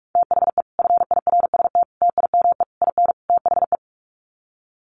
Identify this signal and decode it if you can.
Morse code, heard as a keyed tone, about 46 wpm, 700 Hz. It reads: THE FIRST TIME IN THE
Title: Sets of 5 Words Encore 45wpm